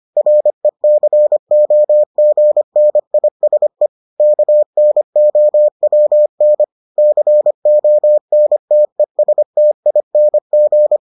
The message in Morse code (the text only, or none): RECOGNISE KNOWN CONTESTING